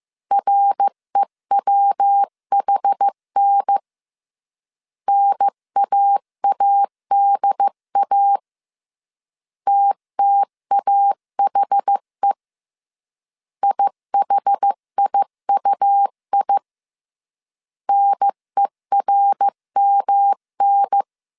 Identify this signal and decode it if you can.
Morse code, heard as a keyed tone, about 15 wpm, 800 Hz. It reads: REWHN NAADA TTAHE IHIUI NERMN
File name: CW / DET / 1 3 1500C S A 14wpm 2447